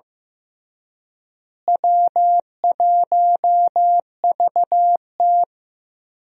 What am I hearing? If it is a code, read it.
Morse code, heard as a keyed tone, about 15 wpm, 700 Hz. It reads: W1VT
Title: USA Call Signs - 4 Characters - Repeated 3x - 15wpm